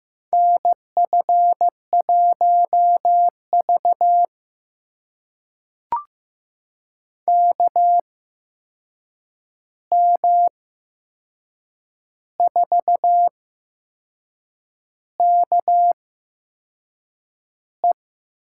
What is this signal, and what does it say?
Morse code, heard as a keyed tone, about 15 wpm, spaced as for 6 wpm, 700 Hz. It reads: NF1V KM4KE
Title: USA Call Signs - 8x Character Spacing - 15wpm